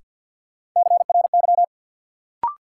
Morse code, heard as a keyed tone, about 52 wpm, 700 Hz. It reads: CRY
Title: Sets of 5 Words Encore - New Word Drill - 50wpm